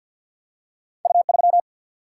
Morse code, heard as a keyed tone, about 53 wpm, 700 Hz. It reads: U3
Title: Single-Letter Single-Number 50wpm